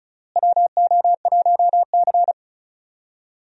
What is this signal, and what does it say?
Morse code, heard as a keyed tone, about 36 wpm, 700 Hz. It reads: WO1C
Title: USA Call Signs - 4 Characters - Repeated 3x - 35wpm